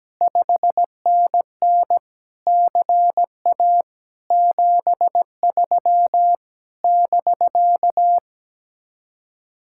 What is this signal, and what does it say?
Morse code, heard as a keyed tone, about 17 wpm, 700 Hz. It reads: 5NN CA 73 <BK>
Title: Speed-Racing Common QSO Phrases Encore - 17wpm